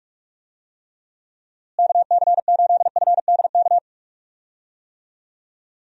Morse code, heard as a keyed tone, about 41 wpm, 700 Hz. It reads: KC8FDK
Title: USA Call Signs - 6 Characters - Repeated 3x - 45wpm